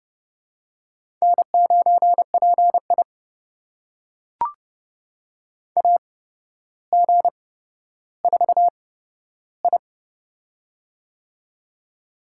Morse code, heard as a keyed tone, about 30 wpm, spaced as for 13 wpm, 700 Hz. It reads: N9PI AG4I
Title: USA Call Signs - 4 Characters - 8x Character Spacing - 30wpm